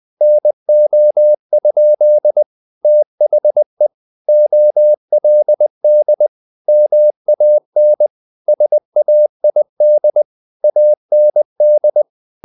Morse code, heard as a keyed tone, about 20 wpm, 600 Hz. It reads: NO? THE OLD MAN SAID AND
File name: the old man and the sea part 2 20wpm